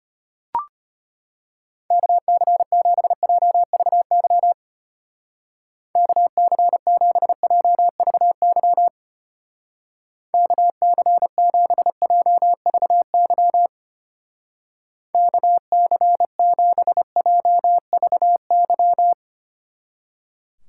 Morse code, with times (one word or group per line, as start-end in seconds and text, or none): 1.9-4.5 KC7JVY
6.0-8.9 KC7JVY
10.3-13.7 KC7JVY
15.1-19.1 KC7JVY